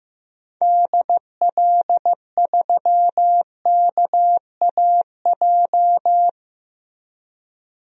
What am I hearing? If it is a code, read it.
Morse code, heard as a keyed tone, about 15 wpm, 700 Hz. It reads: DL3KAJ
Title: DXCC-100 Call Signs - 6 Characters - 15wpm